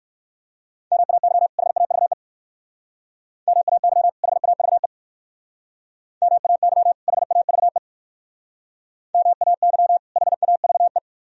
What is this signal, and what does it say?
Morse code, heard as a keyed tone, about 59 wpm, 700 Hz. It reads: MAY HAVE MAY HAVE MAY HAVE MAY HAVE